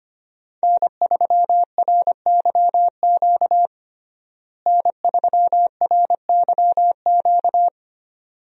Morse code, heard as a keyed tone, about 25 wpm, 700 Hz. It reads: N3RYQ N3RYQ
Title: USA Call Signs - 5 Characters - Repeated 2x - 25wpm